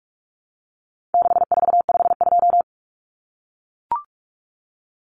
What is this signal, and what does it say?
Morse code, heard as a keyed tone, about 44 wpm, 700 Hz. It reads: 6452
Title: Common Cut Numbers - 4-Digit 45wpm